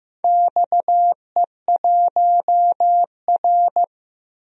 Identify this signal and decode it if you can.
Morse code, heard as a keyed tone, about 15 wpm, 700 Hz. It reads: XE1R
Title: DXCC-100 Call Signs - 4 Characters - 15wpm